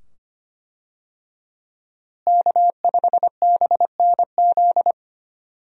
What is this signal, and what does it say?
Morse code, heard as a keyed tone, about 25 wpm, 700 Hz. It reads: K5BNZ
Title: USA Call Signs - 5 Characters - Repeated 2x - 25wpm